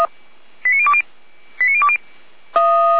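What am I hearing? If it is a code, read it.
Five-tone selective call (ZVEI), heard as quick runs of five tones, tones of about 80 ms. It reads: 89020 89020
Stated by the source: ZVEI five-tone selective call received on an SDR.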